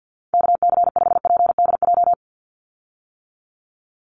Morse code, heard as a keyed tone, about 50 wpm, 700 Hz. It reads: KC5PDJ
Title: USA Call Signs 50wpm